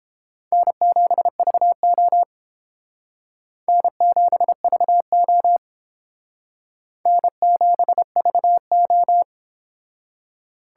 Morse code, heard as a keyed tone, about 28 wpm, 700 Hz. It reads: N7VO N7VO N7VO